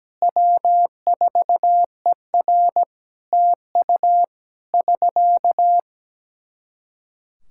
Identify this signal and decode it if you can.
Morse code, heard as a keyed tone, about 17 wpm, 700 Hz. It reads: W4ER TU <SK>
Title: Practice QSO - Exchange 4 - Ending - 17wpm